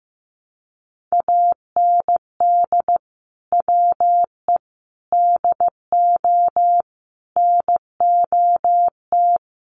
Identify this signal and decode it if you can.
Morse code, heard as a keyed tone, about 15 wpm, 700 Hz. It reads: AND WE DO NOT
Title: Sets of 5 Words Encore 15wpm